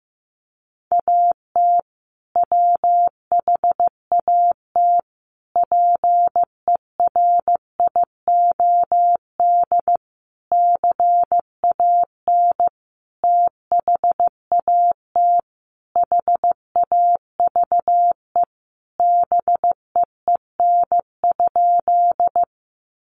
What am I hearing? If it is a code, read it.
Morse code, heard as a keyed tone, about 15 wpm, 700 Hz. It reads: AT WHAT PERIOD CAN THAT HAVE BEEN?
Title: Sentences from Top 800 Words 15wpm